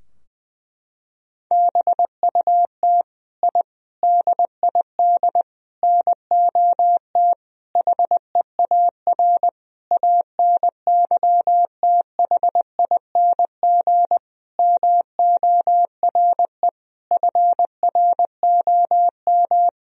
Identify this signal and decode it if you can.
Morse code, heard as a keyed tone, about 20 wpm, 700 Hz. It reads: BUT I DID NOT HEAR ANYTHING MORE FROM
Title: Sentences from Top 1000 Words - Review Part 4 - 20wpm